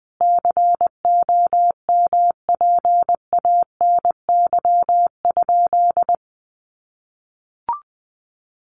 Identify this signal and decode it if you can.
Morse code, heard as a keyed tone, about 20 wpm, 700 Hz. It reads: COMPANY?
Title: Sentences from Top 1000 Words 20wpm